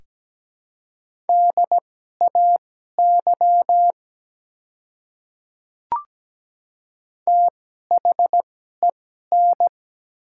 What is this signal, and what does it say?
Morse code, heard as a keyed tone, about 17 wpm, spaced as for 12 wpm, 700 Hz. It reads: DAY THEN
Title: 2x Character Spacing - Top 100 Words - 17wpm